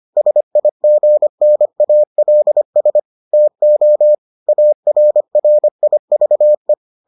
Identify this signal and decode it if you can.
Morse code, heard as a keyed tone, about 25 wpm, 600 Hz. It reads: SIGNALS TO ARRIVE